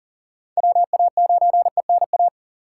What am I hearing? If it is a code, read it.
Morse code, heard as a keyed tone, about 40 wpm, 700 Hz. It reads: WA9ENA
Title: K1USN SST QSO Practice - 40wpm